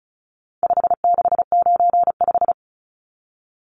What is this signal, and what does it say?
Morse code, heard as a keyed tone, about 34 wpm, 700 Hz. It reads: 5695
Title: Numbers - 4-Digit 35wpm